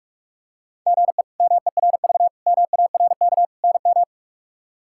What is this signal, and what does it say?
Morse code, heard as a keyed tone, about 43 wpm, 700 Hz. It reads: GE MERV MARK NM
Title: K1USN SST QSO Practice - 45wpm